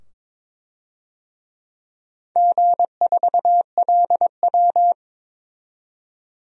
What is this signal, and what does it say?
Morse code, heard as a keyed tone, about 22 wpm, 700 Hz. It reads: G4LW
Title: DXCC-100 Call Signs - 4 Characters - 22wpm